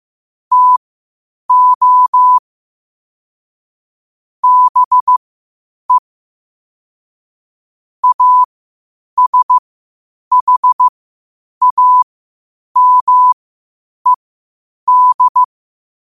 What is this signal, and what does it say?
Morse code, heard as a keyed tone, about 15 wpm, spaced as for 8 wpm, 1000 Hz. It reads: TO BE ASHAMED
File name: qotdfiles / 8wpm